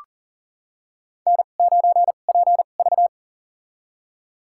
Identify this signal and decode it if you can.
Morse code, heard as a keyed tone, about 40 wpm, spaced as for 29 wpm, 700 Hz. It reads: N9PV
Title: USA Call Signs - 4 Characters - 2x Character Spacing - 40wpm